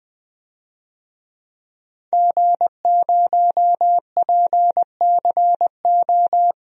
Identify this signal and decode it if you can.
Morse code, heard as a keyed tone, about 20 wpm, 700 Hz. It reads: G0PCO
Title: DXCC-100 Call Signs - 5 Characters - Repeated 3x - 20wpm